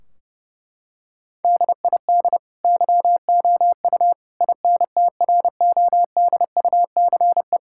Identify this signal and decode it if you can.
Morse code, heard as a keyed tone, about 30 wpm, 700 Hz. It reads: DID YOU INTRODUCE
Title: Sentences from Top 1600 Words 30wpm